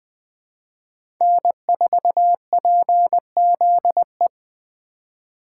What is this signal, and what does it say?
Morse code, heard as a keyed tone, about 20 wpm, 700 Hz. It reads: N4PZE
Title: USA Call Signs - 2x Character Spacing - 20wpm